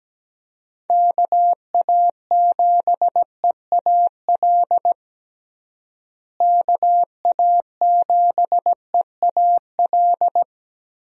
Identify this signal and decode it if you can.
Morse code, heard as a keyed tone, about 17 wpm, 700 Hz. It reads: KA7EAL KA7EAL
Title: USA Call Signs - 6 Characters - Repeated 3x - 17wpm